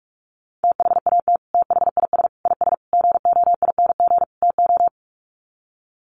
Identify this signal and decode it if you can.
Morse code, heard as a keyed tone, about 44 wpm, 700 Hz. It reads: THAT THIS IS GOING TO
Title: Sets of 5 Words Encore2 45wpm